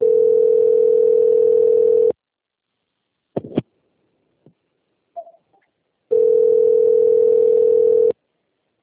Ringback tone (North America), 440 + 480 Hz.